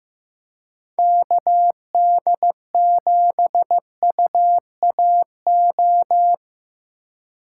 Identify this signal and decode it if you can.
Morse code, heard as a keyed tone, about 18 wpm, 700 Hz. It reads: KD7UAO
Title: Speed Racing - USA 2x3 Call Signs 15wpm